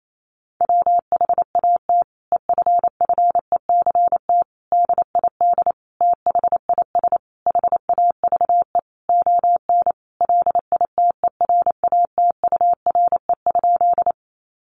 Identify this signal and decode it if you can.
Morse code, heard as a keyed tone, about 28 wpm, 700 Hz. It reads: WHAT EFFECT DID THIS HAVE ON LITERATURE?
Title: Sentences from Top 1000 Words 28wpm